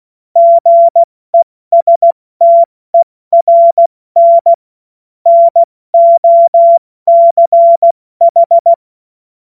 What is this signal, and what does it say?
Morse code, heard as a keyed tone, about 16 wpm, 680 Hz. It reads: GESTERN NOCH
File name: Morsekurs / Witze / Witz 07.November 2023 70 F80